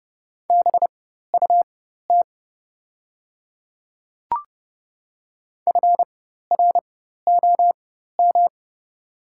Morse code, heard as a keyed tone, about 30 wpm, spaced as for 14 wpm, 690 Hz. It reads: BUT FROM
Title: 4x Character Spacing - Top 100 Words - 30wpm